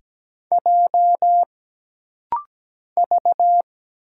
Morse code, heard as a keyed tone, about 17 wpm, 700 Hz. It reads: J V
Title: Single Letter-Number - Rapid-Fire 17wpm